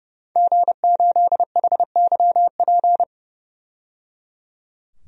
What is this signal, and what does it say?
Morse code, heard as a keyed tone, about 32 wpm, 700 Hz. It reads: G8HYP
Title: DXCC-100 Call Signs - 5 Characters - 30wpm